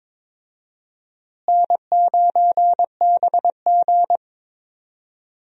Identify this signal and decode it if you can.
Morse code, heard as a keyed tone, about 22 wpm, 700 Hz. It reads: N9BG
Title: USA Call Signs - 4 Characters - Repeated 3x - 22wpm